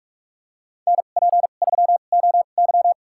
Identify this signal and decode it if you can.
Morse code, heard as a keyed tone, about 46 wpm, spaced as for 33 wpm, 700 Hz. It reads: NP3OY
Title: USA Call Signs - 5 Characters - 2x Character Spacing - 45wpm